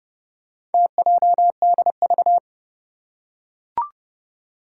Morse code, heard as a keyed tone, about 30 wpm, 700 Hz. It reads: TJDV
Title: Four Challenging Letter-Number Combinations 30wpm